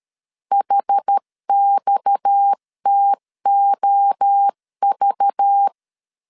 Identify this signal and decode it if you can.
Morse code, heard as a keyed tone, about 13 wpm, 800 Hz. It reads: HXTOV
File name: CW / DET / 1 6 1500C S C 12wpm 3533